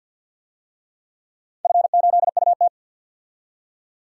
Morse code, heard as a keyed tone, about 53 wpm, 700 Hz. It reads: F8UT
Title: DXCC-100 Call Signs - 4 Characters - 50wpm